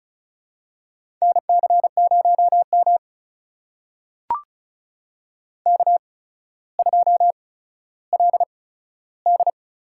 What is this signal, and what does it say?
Morse code, heard as a keyed tone, about 34 wpm, spaced as for 15 wpm, 700 Hz. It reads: NC0M K2LD
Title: USA Call Signs - 4 Characters - 8x Character Spacing - 35wpm